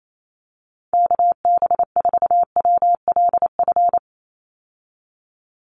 Morse code, heard as a keyed tone, about 28 wpm, 700 Hz. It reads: KB4WLF
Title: Call Signs Repeated 3x - 28wpm